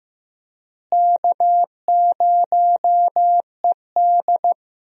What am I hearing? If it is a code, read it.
Morse code, heard as a keyed tone, about 15 wpm, 700 Hz. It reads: K0ED
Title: USA Call Signs - 4 Characters - Repeated 3x - 15wpm